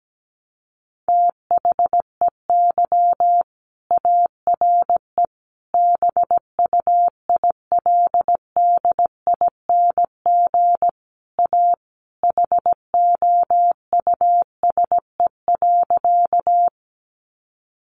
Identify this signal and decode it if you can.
Morse code, heard as a keyed tone, about 17 wpm, 700 Hz. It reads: THEY ARE BUILDING A HOUSE.
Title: Sentences from Top 700 Words 17wpm